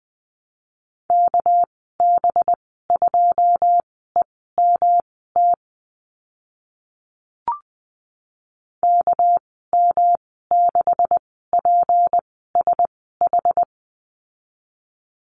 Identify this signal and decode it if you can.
Morse code, heard as a keyed tone, about 20 wpm, spaced as for 14 wpm, 700 Hz. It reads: KB2EMT KM6PSH
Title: 200 Lesson 32g - Letter k - Call Signs 20wpm